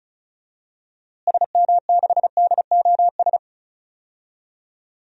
Morse code, heard as a keyed tone, about 36 wpm, 700 Hz. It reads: SM6DOS
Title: DXCC-100 Call Signs - 6 Characters - Repeated 2x - 35wpm